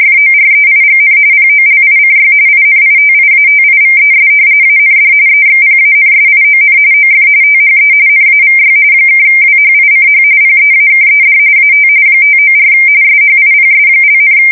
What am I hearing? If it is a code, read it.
RTTY, about 45.45 baud. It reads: FOREIGN POLICY INITIATIVES AND EXPANSION OF THE NAVY (WHICH PEAKED AT 594 SHIPS),
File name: fsk170 / battleshipnj